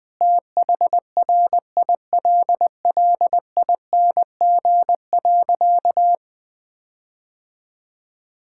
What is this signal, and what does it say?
Morse code, heard as a keyed tone, about 20 wpm, 700 Hz. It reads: THRILLING.
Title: Difficult Sentences with Hints - 20wpm